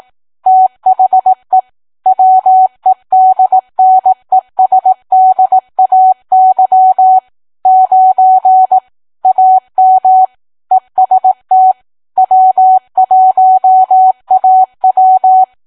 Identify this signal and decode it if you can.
Morse code, heard as a keyed tone, about 18 wpm, 750 Hz. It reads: THE WEDNESDAY 9 AM EST W1AW